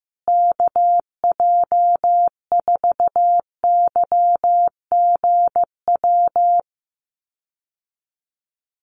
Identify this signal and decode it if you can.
Morse code, heard as a keyed tone, about 15 wpm, 700 Hz. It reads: KJ4YGW
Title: USA Call Signs Repeated 3x 15wpm